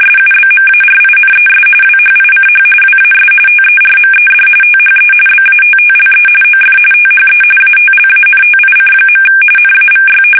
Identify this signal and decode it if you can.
RTTY, about 45.45 baud. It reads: SUBMARINES WHICH WAS TERMED THE BATTLE OF THE ATLANTIC. NAVY
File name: fsk850